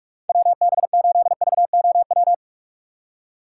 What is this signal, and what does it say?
Morse code, heard as a keyed tone, about 46 wpm, 700 Hz. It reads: WB8VOW
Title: USA Call Signs - 6x Character Spacing - 45wpm